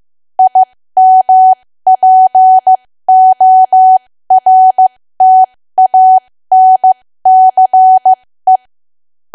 Morse code, heard as a keyed tone, about 15 wpm, spaced as for 13 wpm, 750 Hz. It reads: IMPORTANCE
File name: cwa / W1AW-Level2 / 13WPM